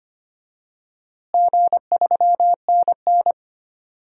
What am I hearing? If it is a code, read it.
Morse code, heard as a keyed tone, about 25 wpm, 700 Hz. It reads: G3NN
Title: DXCC-100 Call Signs - 4 Characters - Repeated 2x - 25wpm